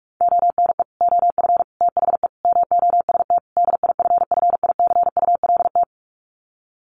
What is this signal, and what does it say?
Morse code, heard as a keyed tone, about 44 wpm, 700 Hz. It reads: ONE OF THE MOST DIFFICULT
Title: Sets of 5 Words Encore2 45wpm